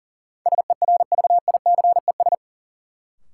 Morse code, heard as a keyed tone, about 40 wpm, 700 Hz. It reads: SERVICES
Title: Sets of 5 Words - New Word Drill - 40wpm